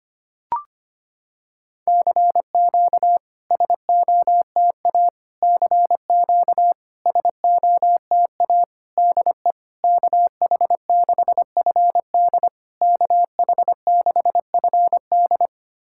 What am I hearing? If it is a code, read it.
Morse code, heard as a keyed tone, about 25 wpm, 700 Hz. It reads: CQ SOTA CQ SOTA DE KH6FD KH6FD
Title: SOTA QSO Practice 25wpm